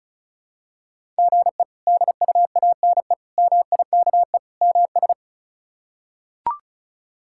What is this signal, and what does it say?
Morse code, heard as a keyed tone, about 34 wpm, 700 Hz. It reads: GE DUANE MIKE MS
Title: K1USN SST QSO Practice - 35wpm